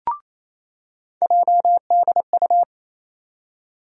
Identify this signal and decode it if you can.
Morse code, heard as a keyed tone, about 28 wpm, 700 Hz. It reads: JDU
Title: USA Call Sign Postfixes - 3 Letters 28wpm